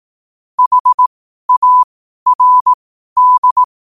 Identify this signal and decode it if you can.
Morse code, heard as a keyed tone, about 18 wpm, spaced as for 12 wpm, 1000 Hz. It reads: HARD
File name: qotdfiles / 12wpm